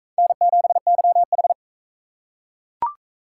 Morse code, heard as a keyed tone, about 41 wpm, 700 Hz. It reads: N7YH